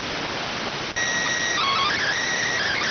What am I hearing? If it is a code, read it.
SSTV (slow-scan television), PD 290.